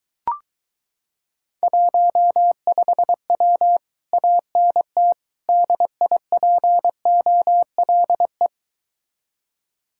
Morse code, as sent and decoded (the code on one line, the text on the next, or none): .---- ..... .-- / .- -. - / -.. .. .--. --- .-.. .
15W ANT DIPOLE